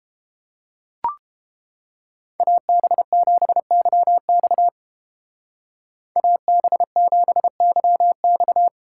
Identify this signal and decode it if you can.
Morse code, heard as a keyed tone, about 31 wpm, 700 Hz. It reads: AB7YX AB7YX